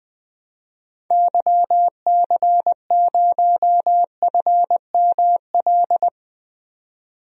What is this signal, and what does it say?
Morse code, heard as a keyed tone, about 20 wpm, 700 Hz. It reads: YC0FML
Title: DXCC-100 Call Signs - 6 Characters - 20wpm